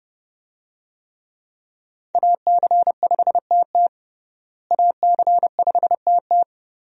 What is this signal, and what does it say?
Morse code, heard as a keyed tone, about 31 wpm, 700 Hz. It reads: AC5TT AC5TT
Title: USA Call Signs - 5 Characters - Repeated 3x - 30wpm